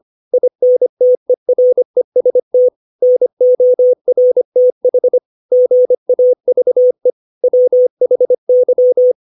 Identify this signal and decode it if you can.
Morse code, heard as a keyed tone, about 25 wpm, 500 Hz. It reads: INTEREST NORTH GAVE WHY
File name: 632 words / 632 Words 25WPM